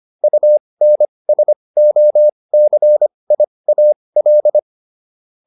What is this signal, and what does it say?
Morse code, heard as a keyed tone, about 25 wpm, spaced as for 20 wpm, 600 Hz. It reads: UNSOCIAL